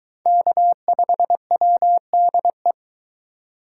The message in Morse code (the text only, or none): K5WDE